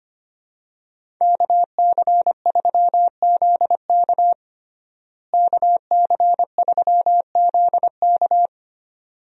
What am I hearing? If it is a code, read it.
Morse code, heard as a keyed tone, about 25 wpm, 700 Hz. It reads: KC3ZK KC3ZK
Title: USA Call Signs - 5 Characters - Repeated 2x - 25wpm